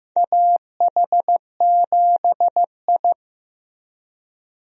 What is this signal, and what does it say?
Morse code, heard as a keyed tone, about 15 wpm, 700 Hz. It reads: AH7I
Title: USA Call Signs - 4 Characters - Repeated 3x - 15wpm